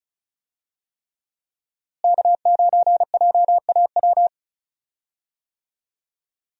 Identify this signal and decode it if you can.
Morse code, heard as a keyed tone, about 36 wpm, 700 Hz. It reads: K9JAW
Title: USA Call Signs - 5 Characters - Repeated 3x - 35wpm